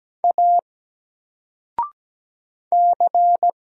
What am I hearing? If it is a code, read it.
Morse code, heard as a keyed tone, about 17 wpm, 700 Hz. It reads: A C
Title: Single Letter-Number Alt 17wpm